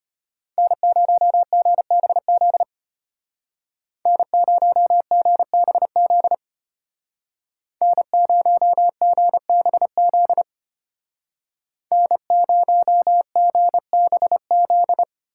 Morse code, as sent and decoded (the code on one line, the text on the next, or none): -. ----- --. -... --.. / -. ----- --. -... --.. / -. ----- --. -... --.. / -. ----- --. -... --..
N0GBZ N0GBZ N0GBZ N0GBZ